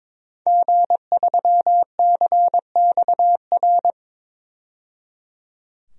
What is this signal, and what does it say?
Morse code, heard as a keyed tone, about 21 wpm, 700 Hz. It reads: G3CXR
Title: DXCC-100 Call Signs - 5 Characters - 22wpm